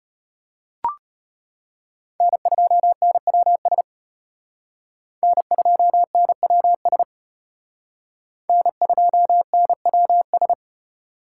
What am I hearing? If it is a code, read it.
Morse code, heard as a keyed tone, about 32 wpm, 700 Hz. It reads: N2NWS N2NWS N2NWS